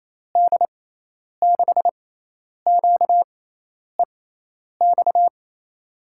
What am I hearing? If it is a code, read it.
Morse code, heard as a keyed tone, about 28 wpm, spaced as for 10 wpm, 700 Hz. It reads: D6QEX
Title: USA Call Signs - 6 Characters - 6x Character Spacing - 28wpm